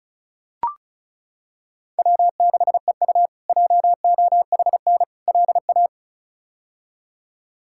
Morse code, heard as a keyed tone, about 34 wpm, 700 Hz. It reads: W6EU JOHN LA
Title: K1USN SST QSO Practice - 35wpm